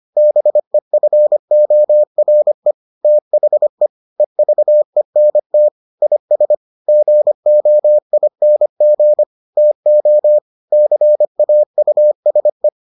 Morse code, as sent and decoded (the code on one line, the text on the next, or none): -... ..-. --- .-. / - .... / . ...- . -. - / .. ... / --. --- .. -. --. / - --- / -.-. .- ..- ... .
BEFORE THE EVENT IS GOING TO CAUSE